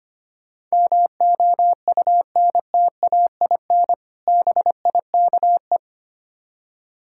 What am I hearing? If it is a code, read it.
Morse code, heard as a keyed tone, about 25 wpm, 700 Hz. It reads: MOUNTAIN BIKE
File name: practice / 25wpm / Difficult Words with Hints - 25wpm